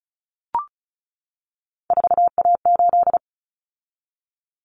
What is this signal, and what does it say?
Morse code, heard as a keyed tone, about 34 wpm, 700 Hz. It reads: 4A8
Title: Common Cut Numbers - 3-Digit 35wpm